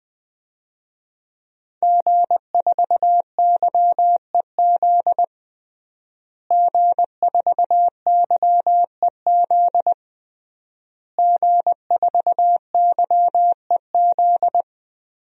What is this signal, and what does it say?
Morse code, heard as a keyed tone, about 20 wpm, 700 Hz. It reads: G4YEZ G4YEZ G4YEZ
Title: DXCC-100 Call Signs - 5 Characters - Repeated 3x - 20wpm